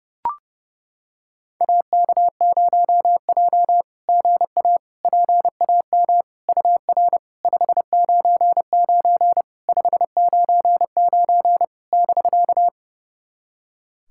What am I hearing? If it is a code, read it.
Morse code, heard as a keyed tone, about 30 wpm, 690 Hz. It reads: AK0J GA PAM UR 599 599 <BK>
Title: SOTA QSO Practice 30wpm